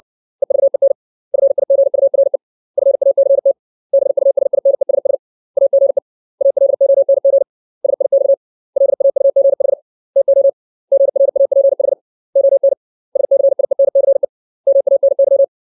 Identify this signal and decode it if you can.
Morse code, heard as a keyed tone, about 60 wpm, 550 Hz. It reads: EVEN FIGURE FACT BUSINESS AGE ALONG SIX LAUGH TO GRAPH ON SCIENCE MANY
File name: 632 words / 632 Words 60WPM